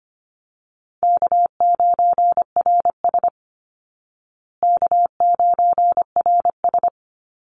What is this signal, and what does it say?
Morse code, heard as a keyed tone, about 25 wpm, 700 Hz. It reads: K9RS K9RS
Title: Easy Call Signs Repeated 3x - 25wpm